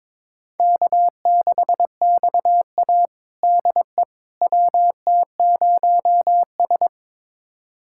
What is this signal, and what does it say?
Morse code, heard as a keyed tone, about 22 wpm, 700 Hz. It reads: K6XA DE WT0S
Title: USA Callsign DE Callsign - 4 Characters - 22wpm